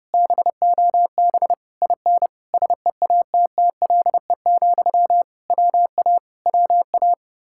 Morse code, heard as a keyed tone, about 33 wpm, 700 Hz. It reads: BOB IN SEATTLE, WA WA